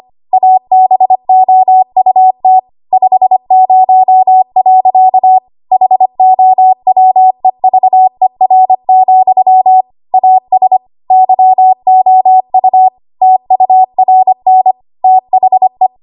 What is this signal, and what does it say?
Morse code, heard as a keyed tone, about 25 wpm, 750 Hz. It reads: ABOUT 50. HOWEVER, AS YOU TURN THE
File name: Archive / 25 WPM / 25WPM